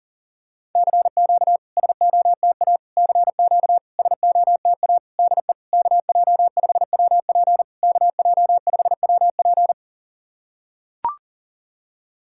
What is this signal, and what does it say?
Morse code, heard as a keyed tone, about 40 wpm, 690 Hz. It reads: CQ SOTA CQ SOTA DE KJ5WP KJ5WP
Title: SOTA QSO Practice 40wpm